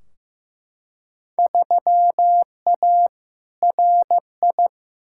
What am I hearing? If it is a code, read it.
Morse code, heard as a keyed tone, about 15 wpm, 700 Hz. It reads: 3A RI
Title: Field Day Exchanges - 15wpm